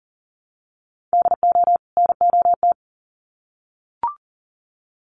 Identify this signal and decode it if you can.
Morse code, heard as a keyed tone, about 40 wpm, 700 Hz. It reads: DO NOT
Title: Sets of 2 Words 40wpm